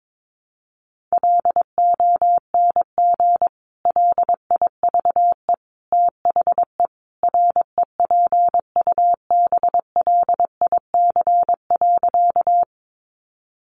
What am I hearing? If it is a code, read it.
Morse code, heard as a keyed tone, about 22 wpm, 700 Hz. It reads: LONG LIVE THE REPUBLIC.
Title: Sentences from Top 1500 Words 22wpm